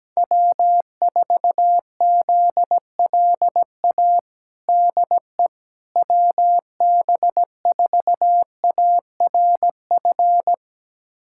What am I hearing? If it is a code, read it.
Morse code, heard as a keyed tone, about 17 wpm, 700 Hz. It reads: W4ZLA DE WB4ARF
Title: USA Callsign DE Callsign 17wpm